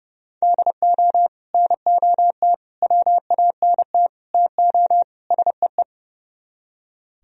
Morse code, heard as a keyed tone, about 38 wpm, 700 Hz. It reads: DO NOT WANT TO SEE